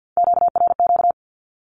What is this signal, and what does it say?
Morse code, heard as a keyed tone, about 48 wpm, 700 Hz. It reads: QRQ
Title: Q-Codes Repeated 3x 50wpm